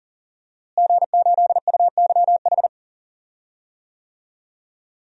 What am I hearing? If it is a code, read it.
Morse code, heard as a keyed tone, about 40 wpm, 700 Hz. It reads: G8UYH
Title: DXCC-100 Call Signs - 5 Characters - Repeated 3x - 40wpm